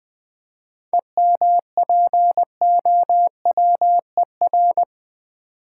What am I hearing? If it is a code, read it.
Morse code, heard as a keyed tone, about 20 wpm, 690 Hz. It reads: EMPOWER
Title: 7 Letter Words - First 3 Letters Given - 20wpm